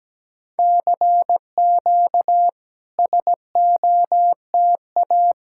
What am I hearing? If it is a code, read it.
Morse code, heard as a keyed tone, about 17 wpm, 700 Hz. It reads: CQ SOTA
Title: SOTA QSO Practice 17wpm